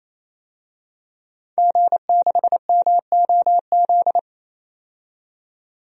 Morse code, heard as a keyed tone, about 28 wpm, 700 Hz. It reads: G6MOZ
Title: DXCC-100 Call Signs - 5 Characters - 28wpm